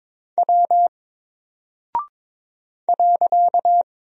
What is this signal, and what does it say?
Morse code, heard as a keyed tone, about 22 wpm, 700 Hz. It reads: W .
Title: Single Character (Letter:Number:Punctuation:Prosign) Alt 22wpm